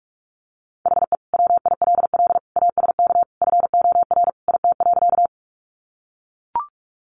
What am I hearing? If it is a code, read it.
Morse code, heard as a keyed tone, about 44 wpm, 700 Hz. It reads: HE WILL ASK FOR IT.
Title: Sentences from Top 200 Words 45wpm